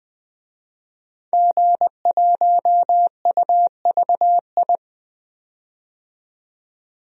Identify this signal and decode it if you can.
Morse code, heard as a keyed tone, about 20 wpm, 700 Hz. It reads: G1UVI
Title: DXCC-100 Call Signs - 5 Characters - 20wpm